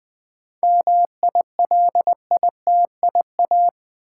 Morse code, heard as a keyed tone, about 20 wpm, 700 Hz. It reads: MILITIA
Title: 7 Letter Words - First 3 Letters Given - 20wpm